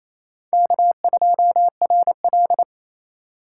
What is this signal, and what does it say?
Morse code, heard as a keyed tone, about 35 wpm, 700 Hz. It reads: K2RL